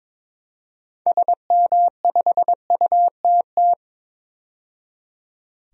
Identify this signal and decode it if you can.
Morse code, heard as a keyed tone, about 22 wpm, 700 Hz. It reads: SM5UTT
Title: DXCC-100 Call Signs - 6 Characters - 22wpm